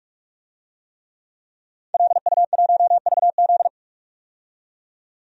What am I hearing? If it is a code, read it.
Morse code, heard as a keyed tone, about 46 wpm, 700 Hz. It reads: LU1VZ
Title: DXCC-100 Call Signs - 5 Characters - 45wpm